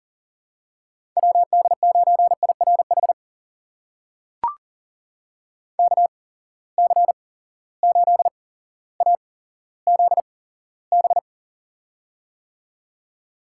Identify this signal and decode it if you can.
Morse code, heard as a keyed tone, about 40 wpm, spaced as for 17 wpm, 700 Hz. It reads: WD9IRH KC8AZB
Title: USA Call Signs - 8x Character Spacing - 40wpm